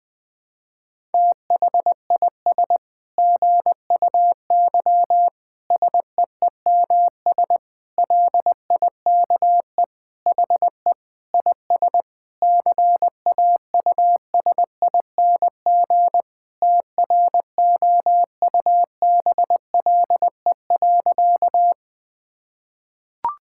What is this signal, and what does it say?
Morse code, heard as a keyed tone, about 20 wpm, 700 Hz. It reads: THIS GUY SEEMS LIKE HE IS CAUSING TROUBLE.
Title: Sentences with 9-Word Phrases - Spoken Beginning - 20wpm